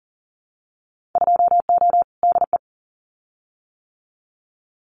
Morse code, heard as a keyed tone, about 40 wpm, 700 Hz. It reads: 2O DE
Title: Winter Field Day Exchanges Encore 40wpm